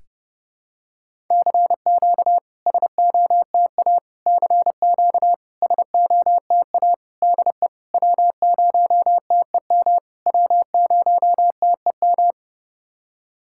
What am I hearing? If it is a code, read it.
Morse code, heard as a keyed tone, about 30 wpm, 690 Hz. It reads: CQ SOTA CQ SOTA DE W0TEM W0TEM
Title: SOTA QSO Practice 30wpm